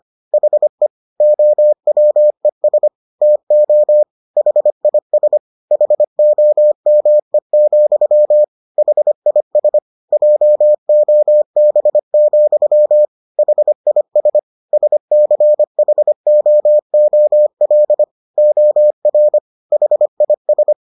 Morse code, heard as a keyed tone, about 25 wpm, 600 Hz. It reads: HE OWES TO HIS HOME, HIS JOB, HIS SCHOOL OR HIS